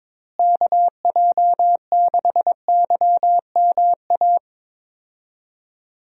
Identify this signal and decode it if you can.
Morse code, heard as a keyed tone, about 21 wpm, 700 Hz. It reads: KJ6YMA
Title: USA Call Signs - 6 Characters - Repeated 3x - 22wpm